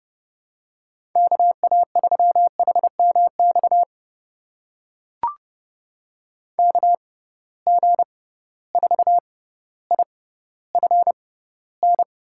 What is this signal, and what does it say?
Morse code, heard as a keyed tone, about 30 wpm, spaced as for 15 wpm, 700 Hz. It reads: KA3HMX KG4IFN
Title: USA Call Signs - 6 Characters - 6x Character Spacing - 30wpm